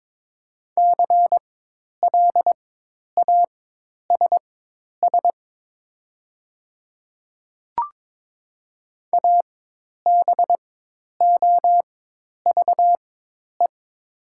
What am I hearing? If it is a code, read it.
Morse code, heard as a keyed tone, about 22 wpm, spaced as for 10 wpm, 700 Hz. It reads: CLASS ABOVE
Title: 4x Character Spacing - Top 500 Words - 22wpm